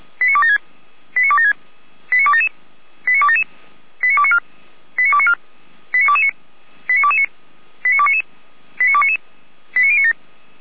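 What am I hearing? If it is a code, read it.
Five-tone selective call (ZVEI), heard as quick runs of five tones, tones of about 70 ms. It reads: 89276 89276 89277 89277 89294 89294 89229 89229 89299 89299 89097
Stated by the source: Received on SDR, ZVEI five-tone call.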